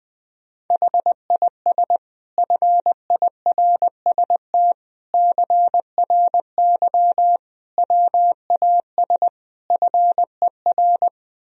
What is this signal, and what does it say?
Morse code, heard as a keyed tone, about 20 wpm, 700 Hz. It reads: HIS FIRST CRY WAS FER
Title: Sentences from Top 500 Words and Abbreviations 20wpm